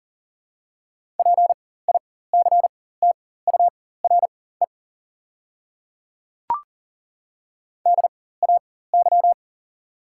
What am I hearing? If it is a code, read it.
Morse code, heard as a keyed tone, about 40 wpm, spaced as for 19 wpm, 700 Hz. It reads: PICTURE DAY
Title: 4x Character Spacing - Top 200 Words - 40wpm